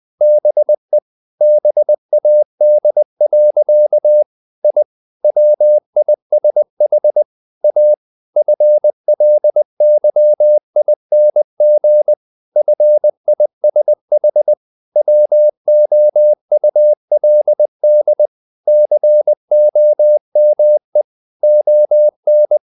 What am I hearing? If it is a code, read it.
Morse code, heard as a keyed tone, about 20 wpm, 600 Hz. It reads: BE BAD. I WISH A FLYING FISH WOULD COME ON
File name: the old man and the sea / the old man and the sea part 1 20wpm